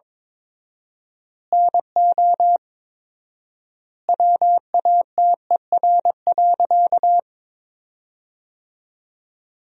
Morse code, heard as a keyed tone, about 21 wpm, 700 Hz. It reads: NO WATER.
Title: 4x Word-Spacing - Sentences from Top 100 Words - 22wpm